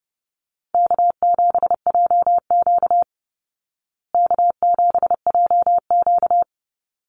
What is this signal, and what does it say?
Morse code, heard as a keyed tone, about 28 wpm, 700 Hz. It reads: K7JQ K7JQ
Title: Call Signs Repeated 3x - 30wpm